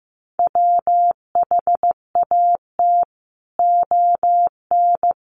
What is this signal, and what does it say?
Morse code, heard as a keyed tone, about 15 wpm, 700 Hz. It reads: WHAT ON
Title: Sentences from Top 700 Words 15wpm